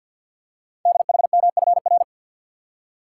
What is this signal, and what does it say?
Morse code, heard as a keyed tone, about 48 wpm, 700 Hz. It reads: DHMFR
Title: Code Groups - 5 Letters - 50wpm